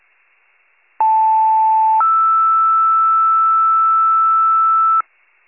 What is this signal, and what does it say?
Two-tone page, heard as a one-second tone, then a three-second tone, about 880 then 1320 Hz, with noise. It is heard from 1.0 to 5.0 s.